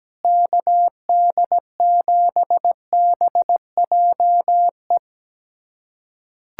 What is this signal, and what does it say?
Morse code, heard as a keyed tone, about 17 wpm, 700 Hz. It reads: KD7BJE
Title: USA Callsign DE Callsign 17wpm